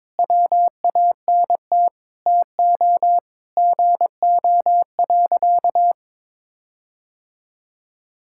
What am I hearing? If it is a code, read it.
Morse code, heard as a keyed tone, about 22 wpm, 700 Hz. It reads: WANT TO GO.
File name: practice / 22wpm / Sentences with 4-Word Phrases - 22wpm